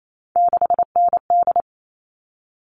Morse code, heard as a keyed tone, about 28 wpm, 700 Hz. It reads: 6ND
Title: All Cut Numbers - 3-Digit 28wpm